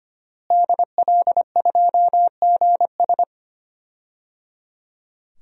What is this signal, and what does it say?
Morse code, heard as a keyed tone, about 25 wpm, 700 Hz. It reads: DL2GS
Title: DXCC-100 Call Signs - 5 Characters - Repeated 3x - 25wpm